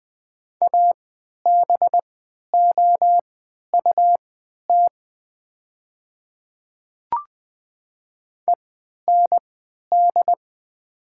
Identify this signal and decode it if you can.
Morse code, heard as a keyed tone, about 20 wpm, spaced as for 11 wpm, 700 Hz. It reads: ABOUT END
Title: 3x Character Spacing - Top 500 Words - 20wpm